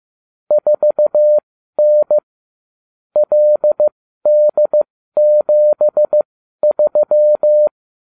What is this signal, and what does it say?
Morse code, heard as a keyed tone, about 15 wpm, spaced as for 12 wpm, 600 Hz. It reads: 4N LD73